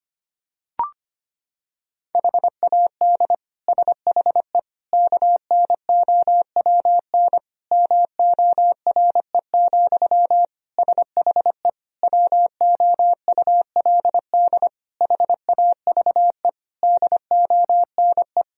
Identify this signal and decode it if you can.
Morse code, heard as a keyed tone, about 25 wpm, 700 Hz. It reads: HAD SHE KNOWN MORE, SHE WOULD HAVE DONE
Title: Sentences from Top 1000 Words - Review Part 3 - 25wpm